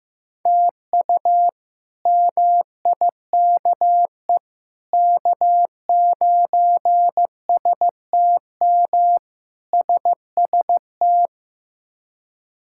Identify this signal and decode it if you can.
Morse code, heard as a keyed tone, about 15 wpm, 700 Hz. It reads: TU MIKE K9STM SST
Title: K1USN SST QSO Practice - 15wpm